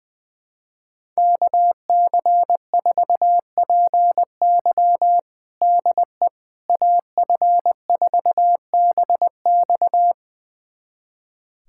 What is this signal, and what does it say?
Morse code, heard as a keyed tone, about 20 wpm, 700 Hz. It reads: KC4PY DE AF4BX
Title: USA Callsign DE Callsign - 5 Characters - 20wpm